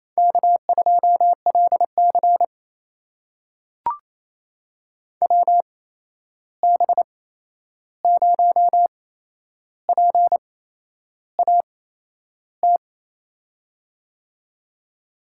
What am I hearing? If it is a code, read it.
Morse code, heard as a keyed tone, about 28 wpm, spaced as for 11 wpm, 700 Hz. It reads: K2LC WB0PAT
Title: USA Call Signs - 8x Character Spacing - 28wpm